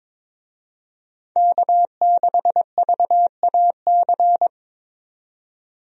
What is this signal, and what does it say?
Morse code, heard as a keyed tone, about 22 wpm, 700 Hz. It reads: K6VAC
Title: USA Call Signs - 5 Characters - 22wpm